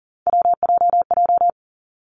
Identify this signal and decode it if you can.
Morse code, heard as a keyed tone, about 38 wpm, 700 Hz. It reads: WJJ
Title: Three Challenging Letter-Number Combinations 40wpm